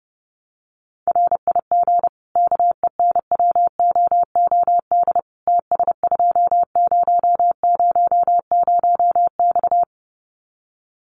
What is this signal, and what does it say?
Morse code, heard as a keyed tone, about 28 wpm, 700 Hz. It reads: RIG KENWOOD TS2000X
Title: Common QSO Phrases 30wpm